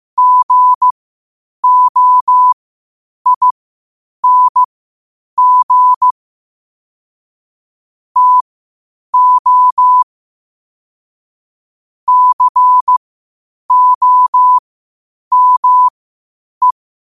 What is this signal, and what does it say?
Morse code, heard as a keyed tone, about 15 wpm, spaced as for 8 wpm, 1000 Hz. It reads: GOING TO COME
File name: qotdfiles / 8wpm